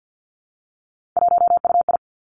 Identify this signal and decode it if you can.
Morse code, heard as a keyed tone, about 50 wpm, 700 Hz. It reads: 1UI
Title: Three Challenging Letter-Number Combinations 50wpm